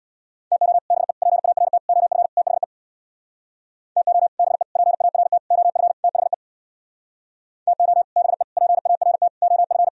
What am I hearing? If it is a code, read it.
Morse code, heard as a keyed tone, about 64 wpm, 700 Hz. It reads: TO BE PART OF THE TO BE PART OF THE TO BE PART OF